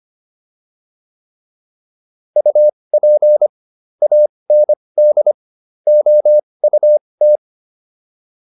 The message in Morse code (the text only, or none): UP AND OUT